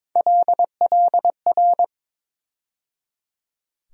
Morse code, heard as a keyed tone, about 22 wpm, 700 Hz. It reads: LLR
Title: USA Call Sign Postfixes - 3 Letters 22wpm